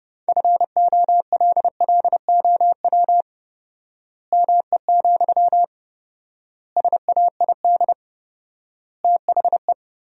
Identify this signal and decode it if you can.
Morse code, heard as a keyed tone, about 30 wpm, 700 Hz. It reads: FOLLOW ME, SAID THE
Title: 4x Word-Spacing - Sentences from Top 200 Words - 30wpm